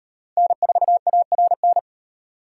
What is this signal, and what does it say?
Morse code, heard as a keyed tone, about 38 wpm, 700 Hz. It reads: N4ARN